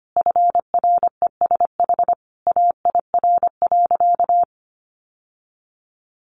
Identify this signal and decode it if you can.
Morse code, heard as a keyed tone, about 25 wpm, 700 Hz. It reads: FRESH AIR.
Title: Sentences from Top 1200 Words 25wpm